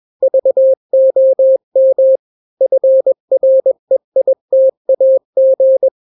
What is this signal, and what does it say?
Morse code, heard as a keyed tone, about 21 wpm, 530 Hz. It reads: VOM FREITAG
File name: daily / 20wpm